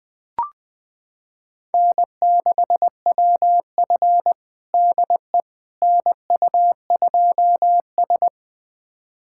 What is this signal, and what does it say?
Morse code, heard as a keyed tone, about 20 wpm, 700 Hz. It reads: N6WF DE NU2S
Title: USA Callsign DE Callsign - 4 Characters - 20wpm